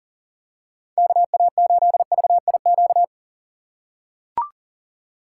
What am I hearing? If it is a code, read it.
Morse code, heard as a keyed tone, about 37 wpm, 700 Hz. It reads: KA8VIQ
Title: USA Call Signs - 6 Characters - 2x Character Spacing - 40wpm